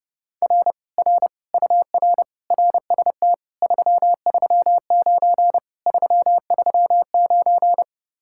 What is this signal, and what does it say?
Morse code, heard as a keyed tone, about 32 wpm, 700 Hz. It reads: R R UR RST 339 339